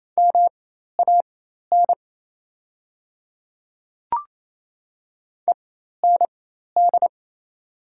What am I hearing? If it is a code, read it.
Morse code, heard as a keyed tone, about 28 wpm, spaced as for 13 wpm, 700 Hz. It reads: MAN END
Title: 4x Character Spacing - Top 300 Words - 28wpm